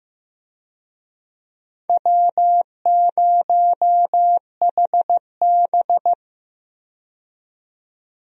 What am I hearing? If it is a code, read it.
Morse code, heard as a keyed tone, about 16 wpm, 700 Hz. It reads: W0HB
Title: USA Call Signs - 4 Characters - 15wpm